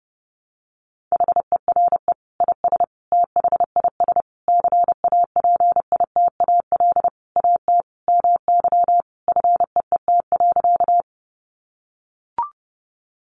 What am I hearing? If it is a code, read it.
Morse code, heard as a keyed tone, about 30 wpm, 700 Hz. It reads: HERE IS THIS CAPITAL AT MY FEET.
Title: Sentences from Top 1000 Words 30wpm